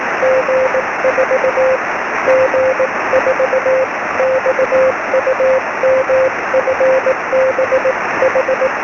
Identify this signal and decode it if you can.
Morse code, heard as a keyed tone, about 18 wpm, 520 Hz, with noise. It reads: G4 G4 XUMFBH
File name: sm7lcb / g4xum 23cm